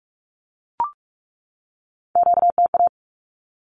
Morse code, heard as a keyed tone, about 44 wpm, 700 Hz. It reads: QTA
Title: Q-Codes 45wpm